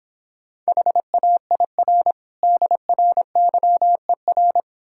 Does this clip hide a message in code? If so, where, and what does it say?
Yes: Morse code from 0.7 to 4.6 s, heard as a keyed tone, about 26 wpm, 690 Hz. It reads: HAIR DRYER